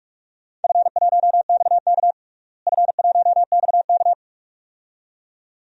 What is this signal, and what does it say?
Morse code, heard as a keyed tone, about 42 wpm, 700 Hz. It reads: F1XK F1XK
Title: DXCC-100 Call Signs - 4 Characters - Repeated 2x - 45wpm